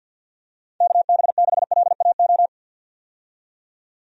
Morse code, heard as a keyed tone, about 46 wpm, 700 Hz. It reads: KB6LAO
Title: USA Call Signs - 6 Characters - 50wpm